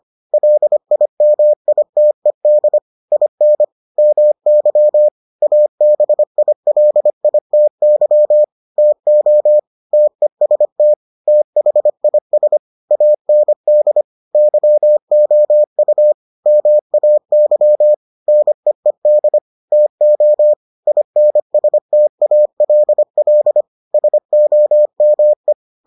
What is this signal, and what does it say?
Morse code, heard as a keyed tone, about 25 wpm, 600 Hz. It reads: LIMITED IN MY ABILITY TO TEST THIS AND YOU MAY NEED TO INSTALL SOME